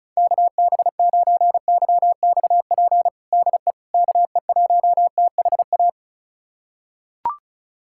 Morse code, heard as a keyed tone, about 34 wpm, 700 Hz. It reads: KB9YXP DE KE1THA
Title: USA Callsign DE Callsign 35wpm